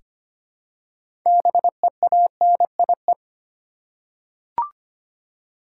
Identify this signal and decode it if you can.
Morse code, heard as a keyed tone, about 25 wpm, 690 Hz. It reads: BEANIE
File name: practice / 25wpm / Easy Words with Hints - 25wpm